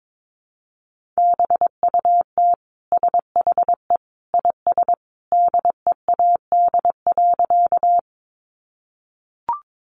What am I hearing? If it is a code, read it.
Morse code, heard as a keyed tone, about 22 wpm, 700 Hz. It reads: BUT SHE IS DEAD.
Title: Sentences from Top 1500 Words 22wpm